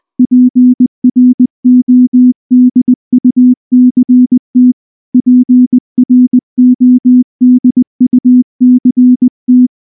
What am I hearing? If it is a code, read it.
Morse code, heard as a keyed tone, about 19 wpm, 250 Hz. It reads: PRODUCT PRODUCT
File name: morse / morse-training-20WPM-9